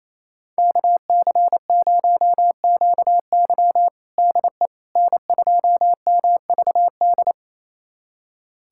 Morse code, heard as a keyed tone, about 28 wpm, 700 Hz. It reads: KC0QY DE N2MVD
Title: USA Callsign DE Callsign - 5 Characters - 28wpm